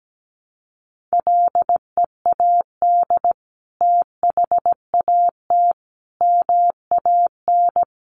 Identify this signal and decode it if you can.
Morse code, heard as a keyed tone, about 17 wpm, 700 Hz. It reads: LEAD THAT MAN
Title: Sentences from Top 600 Words 17wpm